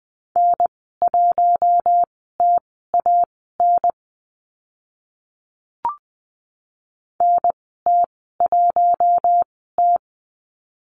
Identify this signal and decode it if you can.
Morse code, heard as a keyed tone, about 20 wpm, spaced as for 14 wpm, 700 Hz. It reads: N1TAN NT1T
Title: 042 Lesson 8g - Number 1 - Call Signs 20wpm